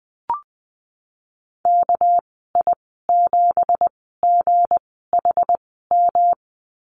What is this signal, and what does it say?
Morse code, heard as a keyed tone, about 20 wpm, spaced as for 15 wpm, 700 Hz. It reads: KI7GHM
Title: 213 Lesson 34g - Number 8 - Call Signs 20wpm